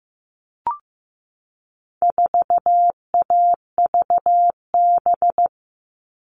Morse code, heard as a keyed tone, about 15 wpm, 700 Hz. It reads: 4AVB
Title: All Cut Numbers - 4-Digit 15wpm